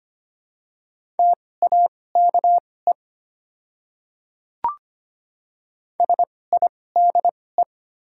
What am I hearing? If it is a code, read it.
Morse code, heard as a keyed tone, about 25 wpm, spaced as for 18 wpm, 700 Hz. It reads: TAKE SIDE
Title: 2x Character Spacing - Top 200 Words - 25wpm